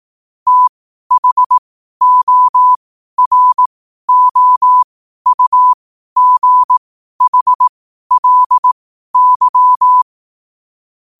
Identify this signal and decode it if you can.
Morse code, heard as a keyed tone, about 18 wpm, spaced as for 12 wpm, 1000 Hz. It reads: THOROUGHLY
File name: qotdfiles / 12wpm